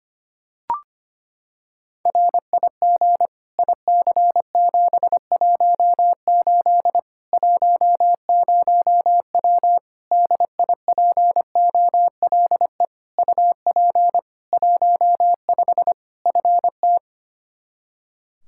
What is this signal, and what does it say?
Morse code, heard as a keyed tone, about 25 wpm, 700 Hz. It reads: RIG IC718 10W DIPOLE UP 15 FT
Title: Practice QSO - Exchange 2 - Info - 25wpm